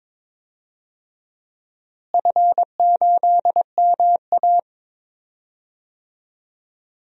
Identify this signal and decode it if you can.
Morse code, heard as a keyed tone, about 22 wpm, 700 Hz. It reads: F8MA
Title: DXCC-100 Call Signs - 4 Characters - Repeated 2x - 22wpm